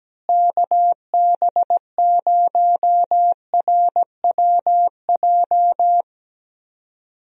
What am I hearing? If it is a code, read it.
Morse code, heard as a keyed tone, about 17 wpm, 700 Hz. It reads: KB0RWJ
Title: USA Call Signs - 6 Characters - Repeated 2x - 17wpm